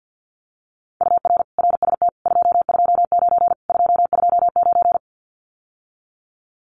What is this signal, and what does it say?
Morse code, heard as a keyed tone, about 50 wpm, 700 Hz. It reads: UR RST 229 229
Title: Common QSO Phrases Encore 50wpm